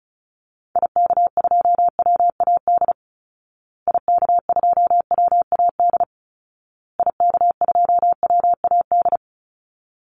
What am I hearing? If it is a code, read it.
Morse code, heard as a keyed tone, about 34 wpm, 700 Hz. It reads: IK2WAD IK2WAD IK2WAD
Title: Call Signs Repeated 2x - 35wpm